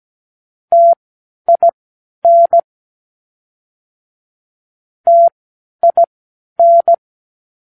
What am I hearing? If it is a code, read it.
Morse code, heard as a keyed tone, about 17 wpm, spaced as for 10 wpm, 680 Hz. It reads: TIN TIN
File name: k7qo / Practicewords